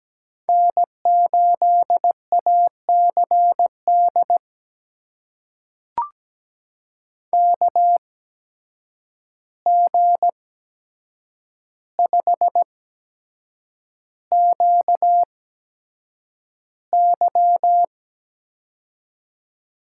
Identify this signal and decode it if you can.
Morse code, heard as a keyed tone, about 17 wpm, spaced as for 7 wpm, 700 Hz. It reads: N8ACD KG5QY
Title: USA Call Signs - 5 Characters - 8x Character Spacing - 17wpm